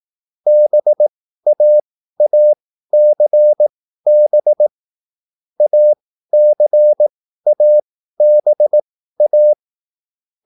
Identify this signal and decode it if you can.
Morse code, heard as a keyed tone, about 18 wpm, spaced as for 13 wpm, 600 Hz. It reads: BAACB ACABA